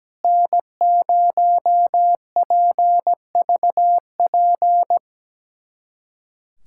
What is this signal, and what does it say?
Morse code, heard as a keyed tone, about 17 wpm, 700 Hz. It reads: N0PVP
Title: USA Callsign DE Callsign - 5 Characters - 17wpm